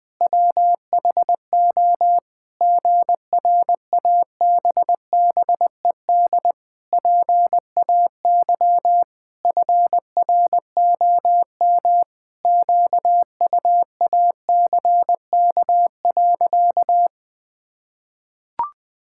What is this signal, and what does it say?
Morse code, heard as a keyed tone, about 20 wpm, 700 Hz. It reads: WHO GRABBED PAY FROM QUACK.
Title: Pangram Challenge v2 - 20wpm